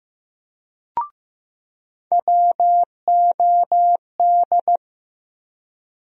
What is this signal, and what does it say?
Morse code, heard as a keyed tone, about 15 wpm, 700 Hz. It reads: WOD
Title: USA Call Sign Postfixes - 3 Letters 15wpm